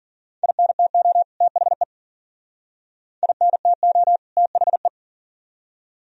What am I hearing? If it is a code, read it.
Morse code, heard as a keyed tone, about 49 wpm, 700 Hz. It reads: INTO THE INTO THE
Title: Speed-Racing - Sets of 2 Words 40wpm